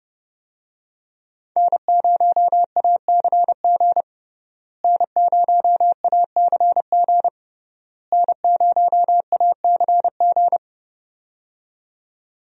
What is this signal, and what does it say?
Morse code, heard as a keyed tone, about 30 wpm, 700 Hz. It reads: N0ACG N0ACG N0ACG
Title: USA Call Signs - 5 Characters - Repeated 3x - 30wpm